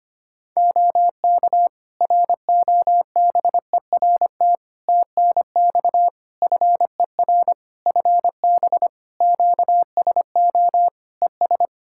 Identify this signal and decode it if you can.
Morse code, heard as a keyed tone, about 25 wpm, 700 Hz. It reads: OK ROBERT TNX FER FB QSO ES
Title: K1USN SST QSO Practice 25wpm